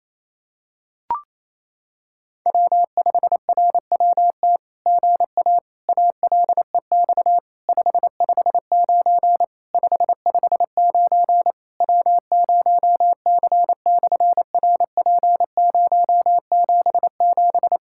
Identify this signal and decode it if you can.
Morse code, heard as a keyed tone, about 28 wpm, 700 Hz. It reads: W5RWT GA ALEX 559 559 W0C/RP077
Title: SOTA QSO Practice 28wpm